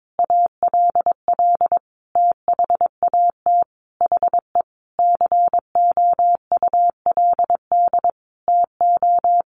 Morse code, heard as a keyed tone, about 22 wpm, 700 Hz. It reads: ALL THAT HE COULD TO
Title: Sentences from Top 1500 Words 22wpm